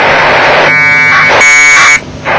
Packet radio (AX.25, as in APRS).
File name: vk3ukf / ISS packet QSO